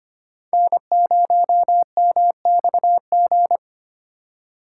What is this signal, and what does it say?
Morse code, heard as a keyed tone, about 25 wpm, 700 Hz. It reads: N0MXG
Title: USA Call Signs - 5 Characters - 4x Character Spacing - 25wpm